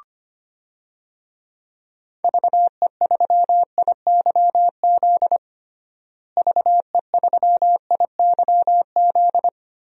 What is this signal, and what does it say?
Morse code, heard as a keyed tone, about 25 wpm, 700 Hz. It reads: VE3IYZ VE3IYZ
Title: DXCC-100 Call Signs - 6 Characters - Repeated 3x - 25wpm